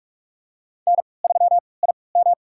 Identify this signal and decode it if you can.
Morse code, heard as a keyed tone, about 45 wpm, spaced as for 25 wpm, 700 Hz. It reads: N3IM
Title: USA Call Signs - 3x Character Spacing - 45wpm